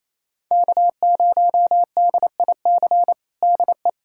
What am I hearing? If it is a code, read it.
Morse code, heard as a keyed tone, about 28 wpm, 700 Hz. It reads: K0DIC DE
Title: USA Callsign DE Callsign - 5 Characters - 28wpm